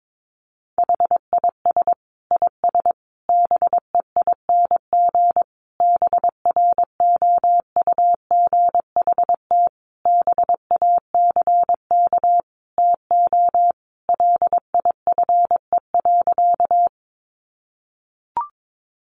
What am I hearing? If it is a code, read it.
Morse code, heard as a keyed tone, about 22 wpm, 700 Hz. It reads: HIS IS BEING BROUGHT BACK TO LIFE.
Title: Sentences from Top 600 Words 22wpm